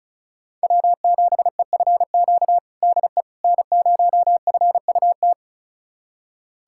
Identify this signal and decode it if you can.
Morse code, heard as a keyed tone, about 36 wpm, 700 Hz. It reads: W7EFQ DE N0FUT
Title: USA Callsign DE Callsign - 5 Characters - 35wpm